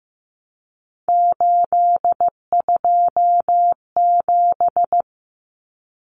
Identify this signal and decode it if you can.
Morse code, heard as a keyed tone, about 15 wpm, 700 Hz. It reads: 827
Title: Common Cut Numbers - 3-Digit 15wpm